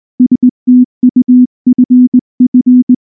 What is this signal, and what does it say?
Morse code, heard as a keyed tone, about 18 wpm, 250 Hz. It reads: STUFF
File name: morse / morse-training-20WPM-1